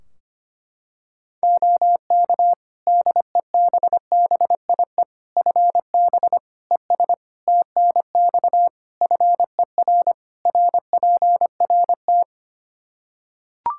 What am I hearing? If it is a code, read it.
Morse code, heard as a keyed tone, about 25 wpm, 700 Hz. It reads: OK DEBBIE FB ES TNX FER RPRT
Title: Practice QSO - Exchange 2 - Info - 25wpm